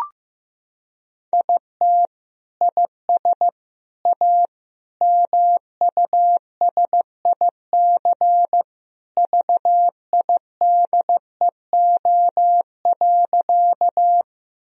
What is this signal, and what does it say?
Morse code, heard as a keyed tone, about 15 wpm, 700 Hz. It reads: IT IS A MUSIC VIDEO.
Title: Sentences from Top 1000 Words - Review Part 4 - 15wpm